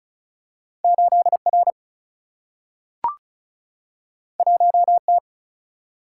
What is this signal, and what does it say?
Morse code, heard as a keyed tone, about 36 wpm, 700 Hz. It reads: 8 R 1 T
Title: Single-Letter Single-Number 35wpm